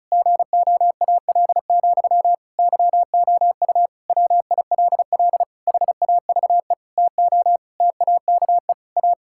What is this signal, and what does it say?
Morse code, heard as a keyed tone, about 34 wpm, 700 Hz. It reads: GOAL, YOU WILL HAVE TO TAKE A
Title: Sentences from Top 1000 Words - Review Part 6 - 35wpm